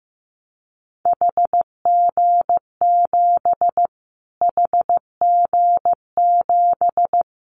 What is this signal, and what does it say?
Morse code, heard as a keyed tone, about 15 wpm, 700 Hz. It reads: HG7 HG7
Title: Callsign Prefixes for Top 25 DXCC - Repeated 3x - 15wpm